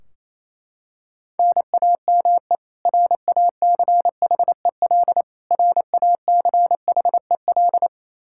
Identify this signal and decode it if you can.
Morse code, heard as a keyed tone, about 28 wpm, 700 Hz. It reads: NAME RACHEL RACHEL
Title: Practice QSO - Exchange 1 - Beginning - 28wpm